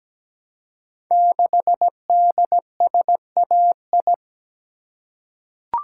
Morse code, heard as a keyed tone, about 17 wpm, 700 Hz. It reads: 6DSAI
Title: Code Groups - 5 Letter:Numbers - 17wpm